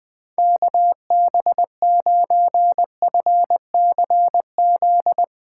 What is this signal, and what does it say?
Morse code, heard as a keyed tone, about 20 wpm, 700 Hz. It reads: KB9FCZ